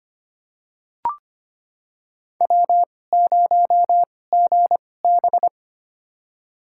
Morse code, heard as a keyed tone, about 25 wpm, spaced as for 18 wpm, 700 Hz. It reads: W0GB
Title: USA Call Signs - 2x Character Spacing - 25wpm